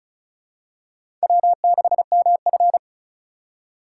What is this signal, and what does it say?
Morse code, heard as a keyed tone, about 36 wpm, 700 Hz. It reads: W6MF
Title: USA Call Signs - 4 Characters - 6x Character Spacing - 35wpm